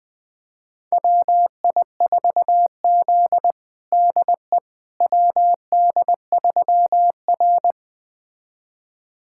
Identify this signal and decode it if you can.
Morse code, heard as a keyed tone, about 21 wpm, 700 Hz. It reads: WI4Z DE WD3R
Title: USA Callsign DE Callsign - 4 Characters - 20wpm